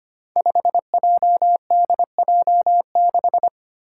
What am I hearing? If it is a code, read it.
Morse code, heard as a keyed tone, about 25 wpm, 700 Hz. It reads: 5JDJ6
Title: Five Challenging Letter-Number Combinations 25wpm